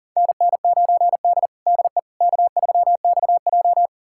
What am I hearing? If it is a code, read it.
Morse code, heard as a keyed tone, about 40 wpm, 700 Hz. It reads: NN9D DE K3XJ
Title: USA Callsign DE Callsign - 4 Characters - 40wpm